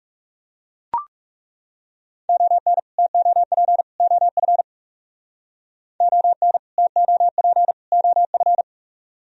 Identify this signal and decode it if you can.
Morse code, heard as a keyed tone, about 37 wpm, 700 Hz. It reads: ON TOP OF ON TOP OF
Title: Speed-Racing - Sets of 3 Words 30wpm